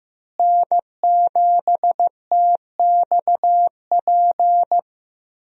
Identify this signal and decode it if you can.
Morse code, heard as a keyed tone, about 15 wpm, 700 Hz. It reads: N7TXP
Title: USA Call Signs - 5 Characters - Repeated 2x - 15wpm